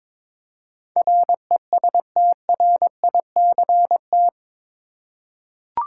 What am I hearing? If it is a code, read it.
Morse code, heard as a keyed tone, about 22 wpm, 700 Hz. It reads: RESTRICT
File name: practice / 22wpm / Eight Letter Words - 22wpm